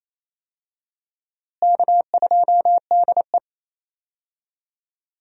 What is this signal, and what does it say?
Morse code, heard as a keyed tone, about 28 wpm, 700 Hz. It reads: K2DE
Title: USA Call Signs - 4 Characters - Repeated 2x - 28wpm